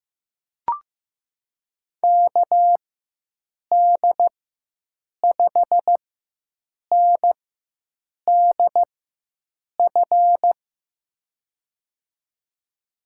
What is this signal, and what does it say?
Morse code, heard as a keyed tone, about 16 wpm, spaced as for 7 wpm, 700 Hz. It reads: KD5NDF
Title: USA Call Signs - 6 Characters - 4x Character Spacing - 15wpm